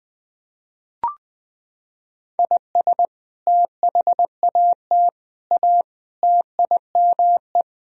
Morse code, heard as a keyed tone, about 20 wpm, 700 Hz. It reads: IS THAT A TIME
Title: Sentences from Top 1000 Words - Review Part 2 - 20wpm